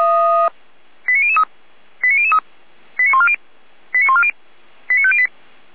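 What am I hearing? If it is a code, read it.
Five-tone selective call (ZVEI), heard as quick runs of five tones, tones of about 70 ms. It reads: 89003 89003 89140 89140 89508
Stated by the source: Five-tone ZVEI selective call 89066, received on an SDR.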